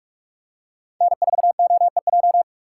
Morse code, heard as a keyed tone, about 46 wpm, 700 Hz. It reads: N4OEJ
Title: USA Call Signs - 5 Characters - 3x Character Spacing - 45wpm